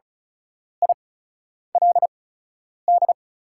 Morse code, heard as a keyed tone, about 36 wpm, spaced as for 10 wpm, 690 Hz. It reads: ILD
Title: 8x Character Spacing - Top 200 Words - 35wpm